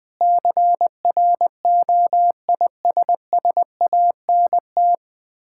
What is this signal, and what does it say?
Morse code, heard as a keyed tone, about 20 wpm, 700 Hz. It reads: CROISSANT
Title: Difficult Words with Hints - 20wpm